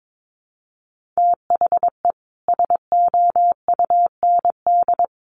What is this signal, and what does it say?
Morse code, heard as a keyed tone, about 22 wpm, 700 Hz. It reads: THE SOUND
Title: Sentences from Top 1400 Words 22wpm